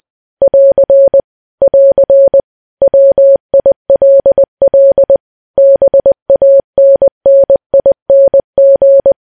Morse code, heard as a keyed tone, about 20 wpm, 550 Hz. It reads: <AR> <AR> WILL BANNING
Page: news-headlines-in-morse-code-at-20-wpm